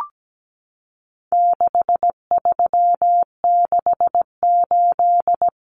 Morse code, heard as a keyed tone, about 17 wpm, 700 Hz. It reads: 6368
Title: Common Cut Numbers - 4-Digit 17wpm